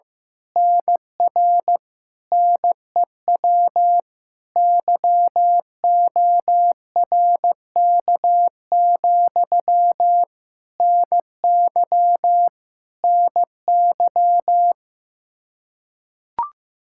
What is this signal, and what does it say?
Morse code, heard as a keyed tone, about 14 wpm, 700 Hz. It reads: NR NEW YORK, NY NY
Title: Practice QSO - Exchange 1 - Beginning - 15wpm